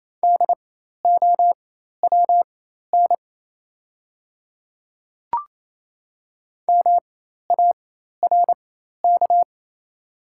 Morse code, heard as a keyed tone, about 28 wpm, spaced as for 13 wpm, 700 Hz. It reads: DOWN MARK
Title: 4x Character Spacing - Top 300 Words - 28wpm